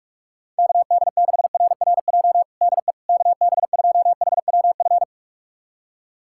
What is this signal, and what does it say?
Morse code, heard as a keyed tone, about 46 wpm, 700 Hz. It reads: KD6RRJ DE KB2HWF
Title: USA Callsign DE Callsign - 6 Characters - 45wpm